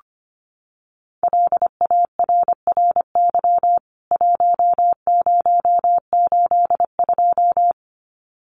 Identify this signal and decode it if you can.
Morse code, heard as a keyed tone, about 25 wpm, 700 Hz. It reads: LARRY 1082
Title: CWT Exchange 25wpm